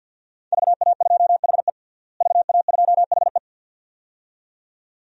Morse code, heard as a keyed tone, about 49 wpm, 700 Hz. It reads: VA2HE VA2HE
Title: DXCC-100 Call Signs - 5 Characters - Repeated 2x - 50wpm